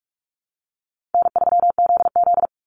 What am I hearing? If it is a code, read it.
Morse code, heard as a keyed tone, about 44 wpm, 700 Hz. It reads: N3ZZ
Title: Easy Call Signs Repeated 2x - 45wpm